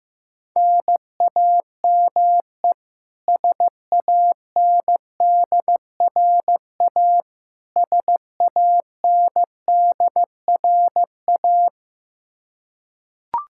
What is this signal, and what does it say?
Morse code, heard as a keyed tone, about 16 wpm, 700 Hz. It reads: NAME SANDRA SANDRA
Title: Practice QSO - Exchange 1 - Beginning - 15wpm